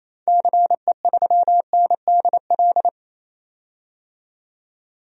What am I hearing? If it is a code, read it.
Morse code, heard as a keyed tone, about 28 wpm, 700 Hz. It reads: CE3NDL
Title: DXCC-100 Call Signs - 6 Characters - 28wpm